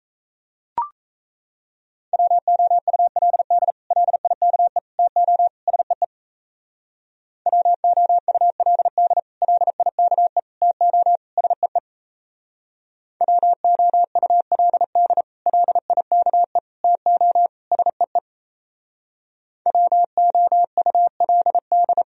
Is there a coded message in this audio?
Yes: Morse code from 2.1 to 22.0 s, heard as a keyed tone, about 35 wpm, 700 Hz. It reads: WOULD LIKE TO SEE WOULD LIKE TO SEE WOULD LIKE TO SEE WOULD